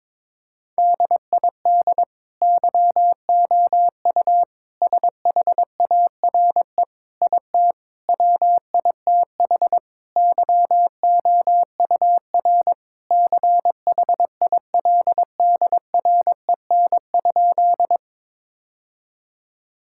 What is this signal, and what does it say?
Morse code, heard as a keyed tone, about 21 wpm, 700 Hz. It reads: DID YOU SHARE IT WITH YOUR CHILDREN?
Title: Sentences from Top 1000 Words - Review Part 6 - 22wpm